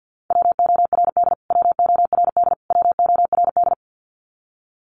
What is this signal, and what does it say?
Morse code, heard as a keyed tone, about 50 wpm, 700 Hz. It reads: WORD WORD WORD
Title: Top 500 Words Repeated 50wpm